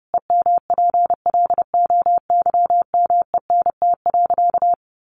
Morse code, heard as a keyed tone, about 30 wpm, 700 Hz. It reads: EMPLOYMENT.
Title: Sentences from Top 1500 Words 30wpm